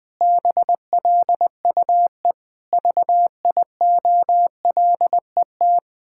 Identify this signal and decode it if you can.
Morse code, heard as a keyed tone, about 21 wpm, 700 Hz. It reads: BLUE VIOLET
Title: Pangram Challenge v2 - 20wpm